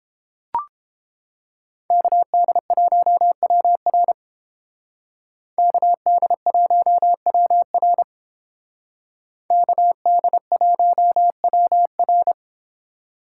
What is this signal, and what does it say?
Morse code, heard as a keyed tone, about 30 wpm, 700 Hz. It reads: KD1WR KD1WR KD1WR